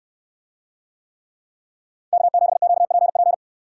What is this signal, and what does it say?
Morse code, heard as a keyed tone, about 69 wpm, 700 Hz. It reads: K7QPP